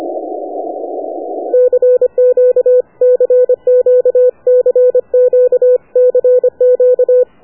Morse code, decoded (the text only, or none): CQ CQ CQ CQ